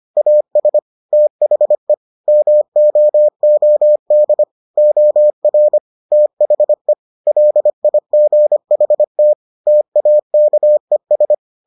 Morse code, heard as a keyed tone, about 25 wpm, 600 Hz. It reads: AS THE MOOD OR THE LIGHT TAKES